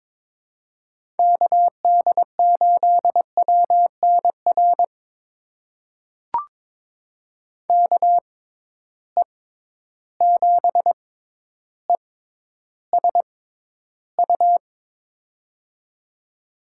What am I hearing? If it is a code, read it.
Morse code, heard as a keyed tone, about 21 wpm, spaced as for 11 wpm, 700 Hz. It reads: KD8WNR KE7ESU
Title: USA Call Signs - 6 Characters - 6x Character Spacing - 22wpm